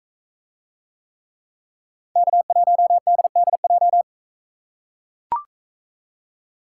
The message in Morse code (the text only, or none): K1DDJ